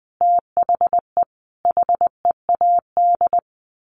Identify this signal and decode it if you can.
Morse code, heard as a keyed tone, about 20 wpm, 700 Hz. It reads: THE HEAD
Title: Sets of 5 Words Encore2 20wpm